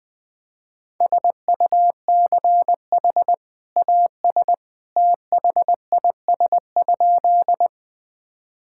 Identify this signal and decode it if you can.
Morse code, heard as a keyed tone, about 21 wpm, 700 Hz. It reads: SUCH AS THIS?
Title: Sentences with 2-Word Phrases - 20wpm